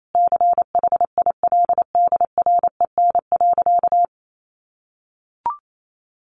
Morse code, heard as a keyed tone, about 28 wpm, 700 Hz. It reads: CHILDREN.
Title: Sentences from Top 900 Words 28wpm